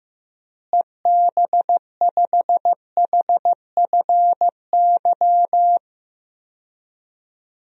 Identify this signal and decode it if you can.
Morse code, heard as a keyed tone, about 16 wpm, 700 Hz. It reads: EB5HFY
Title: DXCC-100 Call Signs - 6 Characters - Repeated 2x - 15wpm